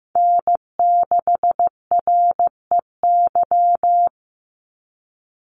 Morse code, heard as a keyed tone, about 15 wpm, 700 Hz. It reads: N6REY
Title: USA Call Signs 15wpm